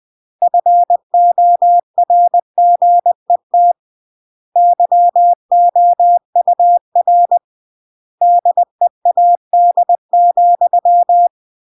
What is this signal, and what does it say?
Morse code, heard as a keyed tone, about 20 wpm, 700 Hz. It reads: FORGET YOUR DEAD,
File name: CW / jodorowsky / 82 commandments 20-20